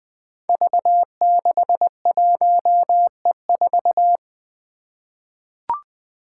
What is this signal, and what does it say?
Morse code, heard as a keyed tone, about 20 wpm, 700 Hz. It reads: V61E4
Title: Five Challenging Letter-Number Combinations 20wpm